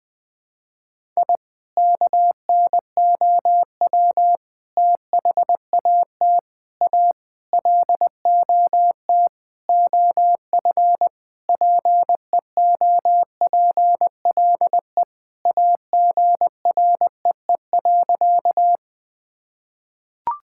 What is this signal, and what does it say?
Morse code, heard as a keyed tone, about 20 wpm, 700 Hz. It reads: I KNOW THAT A LOT OF PEOPLE AGREE.
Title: Sentences with 5-Word Phrases - Spoken Beginning - 20wpm